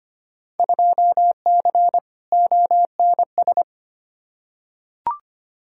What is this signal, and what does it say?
Morse code, heard as a keyed tone, about 25 wpm, 700 Hz. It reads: 2C ONS
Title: Field Day Exchanges Encore - 25wpm